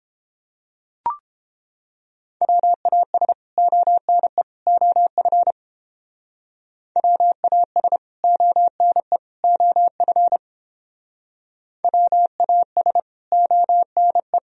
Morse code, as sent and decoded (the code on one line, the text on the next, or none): .-- .- ... / --- -. / --- ..-. / .-- .- ... / --- -. / --- ..-. / .-- .- ... / --- -. .
WAS ONE OF WAS ONE OF WAS ONE